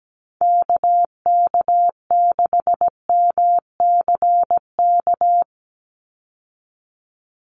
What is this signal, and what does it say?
Morse code, heard as a keyed tone, about 17 wpm, 700 Hz. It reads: KK6MCK
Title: USA Call Signs 17wpm